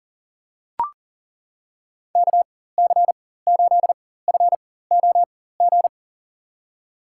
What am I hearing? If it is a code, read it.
Morse code, heard as a keyed tone, about 40 wpm, spaced as for 19 wpm, 700 Hz. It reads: KC8FOG
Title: USA Call Signs - 4x Character Spacing - 40wpm